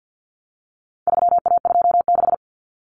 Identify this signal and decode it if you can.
Morse code, heard as a keyed tone, about 50 wpm, 700 Hz. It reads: 3A26
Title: Common Cut Numbers - 4-Digit 50wpm